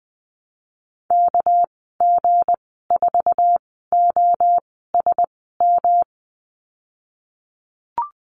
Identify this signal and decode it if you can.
Morse code, heard as a keyed tone, about 20 wpm, spaced as for 15 wpm, 700 Hz. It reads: KG4OSM
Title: 207 Lesson 33g - Letter j - Call Signs 20wpm